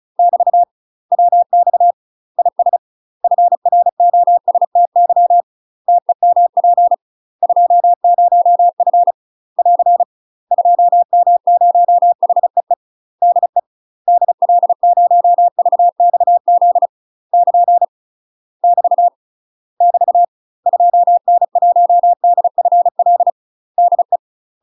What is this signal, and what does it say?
Morse code, heard as a keyed tone, about 36 wpm, 700 Hz. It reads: <BT> WX IS FROSTY TEMP 20F <AR> 2M0HEE DE DL0VXZ <KN> <BT> <BT> 2N1DFL DE
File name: UK MASTER QSOs TRAINING / 35WPM / UK MASTER QSOs 0010